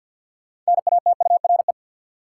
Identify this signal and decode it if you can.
Morse code, heard as a keyed tone, about 52 wpm, 700 Hz. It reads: NATURE
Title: Sentences from Top 600 Words - New Word Drill - 50wpm